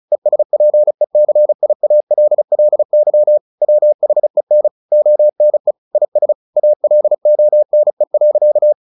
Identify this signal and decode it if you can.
Morse code, heard as a keyed tone, about 35 wpm, 600 Hz. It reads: ESPECIALLY WHEN ONE IS ALONE.
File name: the old man and the sea / the old man and the sea 1 35wpm